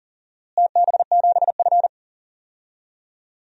Morse code, heard as a keyed tone, about 37 wpm, 700 Hz. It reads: TB7F
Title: DXCC-100 Call Signs - 4 Characters - Repeated 3x - 40wpm